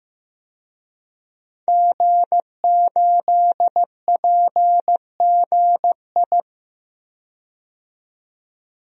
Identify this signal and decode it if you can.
Morse code, heard as a keyed tone, about 15 wpm, 700 Hz. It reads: G8PGI
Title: DXCC-100 Call Signs - 5 Characters - 15wpm